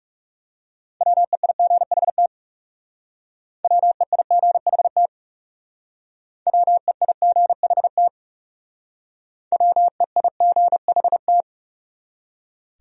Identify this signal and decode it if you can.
Morse code, heard as a keyed tone, about 39 wpm, 700 Hz. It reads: WEIGHT WEIGHT WEIGHT WEIGHT